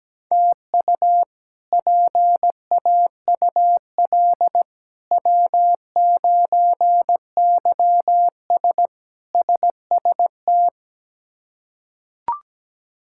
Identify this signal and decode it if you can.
Morse code, heard as a keyed tone, about 17 wpm, 700 Hz. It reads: TU PAUL W9YS SST
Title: K1USN SST QSO Practice - 17wpm